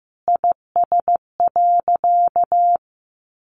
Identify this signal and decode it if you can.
Morse code, heard as a keyed tone, about 14 wpm, 700 Hz. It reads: IS.
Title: Sentences from Top 1500 Words 15wpm